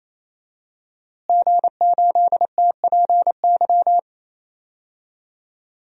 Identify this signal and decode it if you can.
Morse code, heard as a keyed tone, about 28 wpm, 700 Hz. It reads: G8TPY
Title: DXCC-100 Call Signs - 5 Characters - Repeated 3x - 28wpm